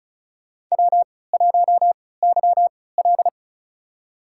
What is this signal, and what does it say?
Morse code, heard as a keyed tone, about 36 wpm, spaced as for 20 wpm, 700 Hz. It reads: W1YL
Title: USA Call Signs - 3x Character Spacing - 35wpm